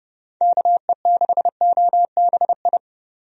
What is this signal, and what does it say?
Morse code, heard as a keyed tone, about 30 wpm, 700 Hz. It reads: KE6OBI
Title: USA Call Signs - 6 Characters - Repeated 3x - 30wpm